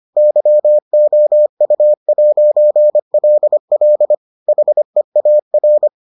Morse code, heard as a keyed tone, about 25 wpm, 600 Hz. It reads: YOU'LL HEAR